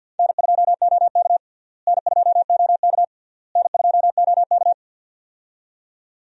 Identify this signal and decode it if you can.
Morse code, heard as a keyed tone, about 52 wpm, 700 Hz. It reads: N2OK N2OK N2OK
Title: USA Call Signs - 4 Characters - Repeated 3x - 50wpm